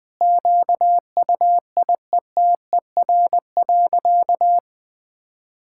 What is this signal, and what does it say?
Morse code, heard as a keyed tone, about 20 wpm, 700 Hz. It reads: QUIETER.
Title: Sentences with 5-Word Phrases - 20wpm